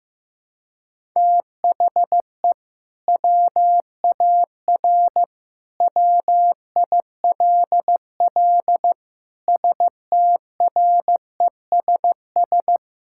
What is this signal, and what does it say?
Morse code, heard as a keyed tone, about 15 wpm, 700 Hz. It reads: THE WAR WILL STRESS
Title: Sentences from Top 1000 Words - Review Part 6 - 15wpm